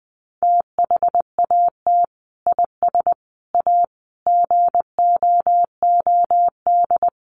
Morse code, heard as a keyed tone, about 19 wpm, 700 Hz. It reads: THAT IS A GOOD
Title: Sentences from Top 700 Words 20wpm